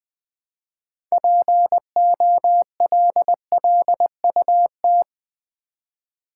Morse code, heard as a keyed tone, about 20 wpm, 700 Hz. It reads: POLLUT
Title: Build A Word - 3 Letters At A Time - 9-Letter Words - 20wpm